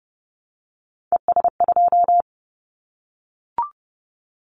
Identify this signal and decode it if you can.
Morse code, heard as a keyed tone, about 30 wpm, 700 Hz. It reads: ES2
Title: Callsign Prefixes for Top 50 DXCC - 30wpm